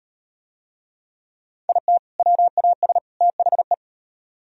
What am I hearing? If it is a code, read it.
Morse code, heard as a keyed tone, about 36 wpm, 700 Hz. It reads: IT WAS THE